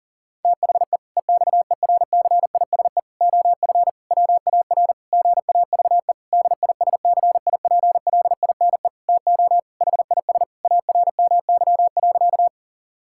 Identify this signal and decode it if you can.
Morse code, heard as a keyed tone, about 40 wpm, 700 Hz. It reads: THE EXERCISE OF WAR GAVE DISCIPLINE TO HIS ARMY.
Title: Sentences from Top 1000 Words - Review Part 6 - 40wpm